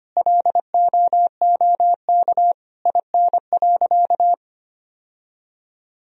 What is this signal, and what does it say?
Morse code, heard as a keyed tone, about 25 wpm, 700 Hz. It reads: LOOK IN.
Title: Sentences from Top 1000 Words - Review Part 2 - 25wpm